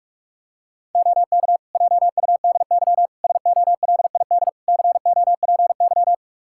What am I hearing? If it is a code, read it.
Morse code, heard as a keyed tone, about 42 wpm, 700 Hz. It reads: OK JUDY SOLID COPY
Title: Practice QSO - Exchange 3 - More Info - 45wpm